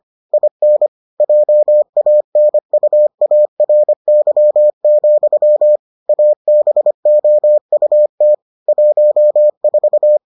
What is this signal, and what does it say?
Morse code, heard as a keyed tone, about 25 wpm, 600 Hz. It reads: IN JANUARY, ABOUT 14